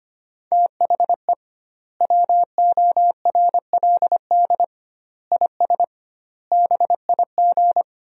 Morse code, heard as a keyed tone, about 25 wpm, 700 Hz. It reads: THE WORLD IS BIG
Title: 2x Word-Spacing - Sentences from Top 400 Words - 25wpm